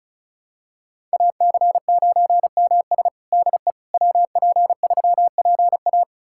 Morse code, heard as a keyed tone, about 36 wpm, 700 Hz. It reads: AC9MS DE WP3PA
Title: USA Callsign DE Callsign - 5 Characters - 35wpm